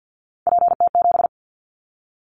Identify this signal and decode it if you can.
Morse code, heard as a keyed tone, about 50 wpm, 700 Hz. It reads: PT7
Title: Callsign Prefixes for Top 100 DXCC - 50wpm